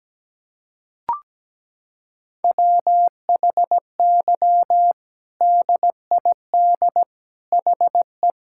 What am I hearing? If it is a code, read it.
Morse code, heard as a keyed tone, about 17 wpm, 700 Hz. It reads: WHY DID HE
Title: Sentences from Top 1000 Words - Review Part 5 - 17wpm